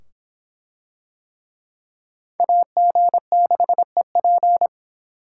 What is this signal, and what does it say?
Morse code, heard as a keyed tone, about 25 wpm, 700 Hz. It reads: AG6EP